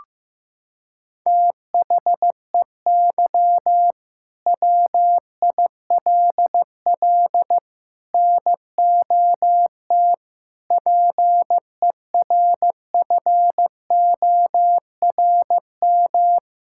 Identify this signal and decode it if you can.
Morse code, heard as a keyed tone, about 15 wpm, 700 Hz. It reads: THEY WILL NOT PERFORM
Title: Sentences from Top 1000 Words - Review Part 5 - 15wpm